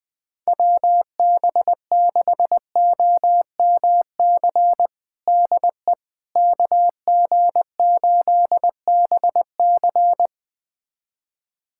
Morse code, heard as a keyed tone, about 20 wpm, 700 Hz. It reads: WB6OMC DE KG8BC
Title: USA Callsign DE Callsign 20wpm